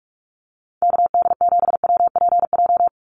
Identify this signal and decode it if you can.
Morse code, heard as a keyed tone, about 44 wpm, 700 Hz. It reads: KD7WPJ
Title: NA SOTA Activator Callsigns 45wpm